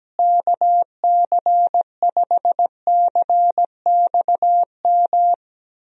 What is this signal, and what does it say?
Morse code, heard as a keyed tone, about 17 wpm, 700 Hz. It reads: KC5CXM
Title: USA Call Signs - 6 Characters - Repeated 3x - 17wpm